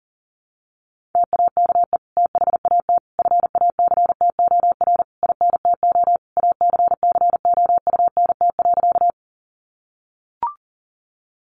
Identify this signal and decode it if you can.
Morse code, heard as a keyed tone, about 40 wpm, 700 Hz. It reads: TAKE THAT FACTOR INTO ACCOUNT.
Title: Sentences from Top 1300 Words 40wpm